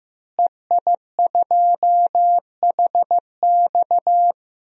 Morse code, heard as a keyed tone, about 15 wpm, 700 Hz. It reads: EI2HX
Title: DXCC-100 Call Signs - 5 Characters - Repeated 2x - 15wpm